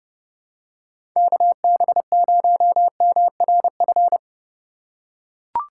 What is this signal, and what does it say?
Morse code, heard as a keyed tone, about 30 wpm, 700 Hz. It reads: KB0MRF
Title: USA Call Signs - 6 Characters - 2x Character Spacing - 30wpm